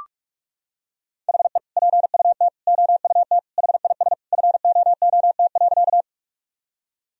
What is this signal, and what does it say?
Morse code, heard as a keyed tone, about 46 wpm, 700 Hz. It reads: HE PUT OUT HIS FOOT.
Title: Sentences from Top 1000 Words - Review Part 3 - 45wpm